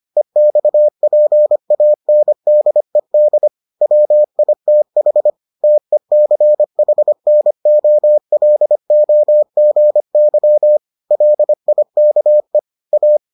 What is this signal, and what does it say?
Morse code, heard as a keyed tone, about 25 wpm, 600 Hz. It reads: EXPANDED WITH TECHNOLOGY LIKE A